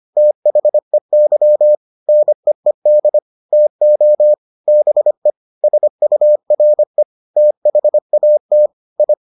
Morse code, heard as a keyed tone, about 25 wpm, 600 Hz. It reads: THEY NEED TO BE SURE THAT I